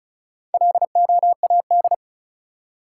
Morse code, heard as a keyed tone, about 34 wpm, 700 Hz. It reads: LOAD
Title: Four Letter Words - 35wpm